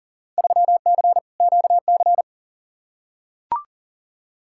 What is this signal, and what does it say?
Morse code, heard as a keyed tone, about 40 wpm, 700 Hz. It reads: 3C QC
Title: Field Day Exchanges Encore - 40wpm